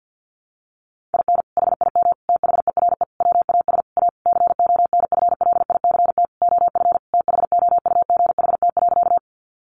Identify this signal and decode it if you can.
Morse code, heard as a keyed tone, about 50 wpm, 700 Hz. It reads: IN HIM THERE WAS A CONFLICT OF THOUGHT.
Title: Sentences from Top 1300 Words 50wpm